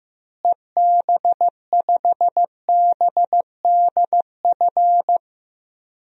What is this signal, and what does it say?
Morse code, heard as a keyed tone, about 16 wpm, 700 Hz. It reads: EB5BDF
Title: DXCC-100 Call Signs - 6 Characters - Repeated 3x - 15wpm